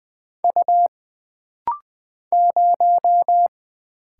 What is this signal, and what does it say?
Morse code, heard as a keyed tone, about 20 wpm, 700 Hz. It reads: U 0
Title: Single Letter-Number Alt - Rapid-Fire 20wpm